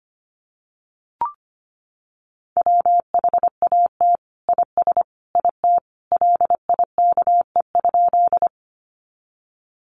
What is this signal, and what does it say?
Morse code, heard as a keyed tone, about 25 wpm, 700 Hz. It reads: WHAT IS IT LIKE?
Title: Sentences from Top 100 Words 25wpm